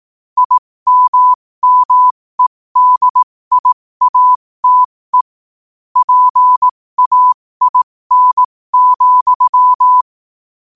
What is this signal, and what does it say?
Morse code, heard as a keyed tone, about 18 wpm, spaced as for 15 wpm, 1000 Hz. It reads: IMMEDIATE PAIN,
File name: qotdfiles / 15wpm